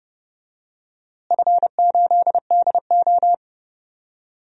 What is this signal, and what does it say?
Morse code, heard as a keyed tone, about 30 wpm, 700 Hz. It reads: F8DO
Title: DXCC-100 Call Signs - 4 Characters - Repeated 3x - 30wpm